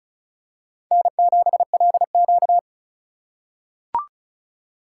Morse code, heard as a keyed tone, about 36 wpm, 700 Hz. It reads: N7LQ
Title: USA Call Signs - 4 Characters - 2x Character Spacing - 35wpm